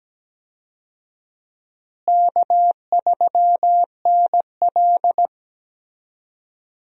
Morse code, heard as a keyed tone, about 17 wpm, 700 Hz. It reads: K3NL
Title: USA Call Signs - 4 Characters - Repeated 3x - 17wpm